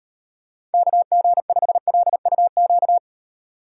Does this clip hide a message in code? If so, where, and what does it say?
Yes: Morse code from 0.7 to 3.0 s, heard as a keyed tone, about 35 wpm, 700 Hz. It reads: KG5LUQ